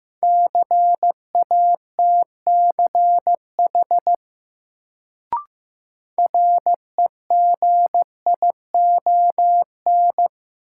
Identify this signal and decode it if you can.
Morse code, heard as a keyed tone, about 15 wpm, 700 Hz. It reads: CATCH REGION
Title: Sentences from Top 1500 Words - New Word Drill - 15wpm